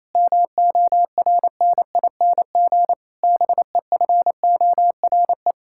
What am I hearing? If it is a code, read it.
Morse code, heard as a keyed tone, about 28 wpm, 700 Hz. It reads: MORNING BEFORE
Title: Pangram Challenge v2 - 28wpm